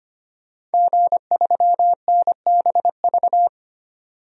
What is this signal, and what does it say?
Morse code, heard as a keyed tone, about 25 wpm, 700 Hz. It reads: G3NBV
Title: DXCC-100 Call Signs - 5 Characters - 25wpm